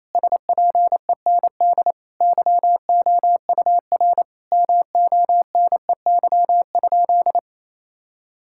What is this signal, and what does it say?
Morse code, heard as a keyed tone, about 28 wpm, 700 Hz. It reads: SPEND YOUR MONEY?
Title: Sentences from Top 1000 Words - Review Part 5 - 28wpm